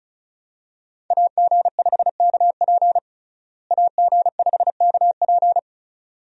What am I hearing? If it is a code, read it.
Morse code, heard as a keyed tone, about 34 wpm, 700 Hz. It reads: AG5KP AG5KP
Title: USA Call Signs - 5 Characters - Repeated 2x - 35wpm